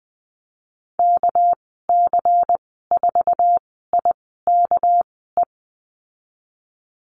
Morse code, heard as a keyed tone, about 20 wpm, spaced as for 14 wpm, 700 Hz. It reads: KC4IKE
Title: 245 Lesson 39g - Letter z - Call Signs 20wpm